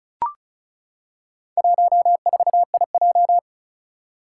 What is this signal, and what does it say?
Morse code, heard as a keyed tone, about 36 wpm, 700 Hz. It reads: 14IJ
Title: Four Challenging Letter-Number Combinations 35wpm